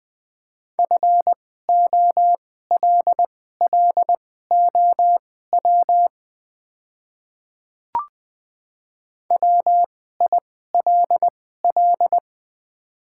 Morse code, heard as a keyed tone, about 21 wpm, spaced as for 15 wpm, 700 Hz. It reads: FOLLOW WILL
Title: 2x Character Spacing - Top 300 Words - 20wpm